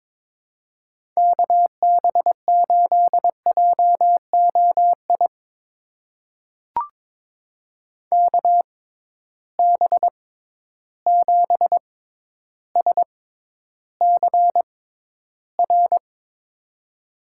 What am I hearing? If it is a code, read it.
Morse code, heard as a keyed tone, about 22 wpm, spaced as for 11 wpm, 700 Hz. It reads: KB8JOI KB7SCR
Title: USA Call Signs - 6x Character Spacing - 22wpm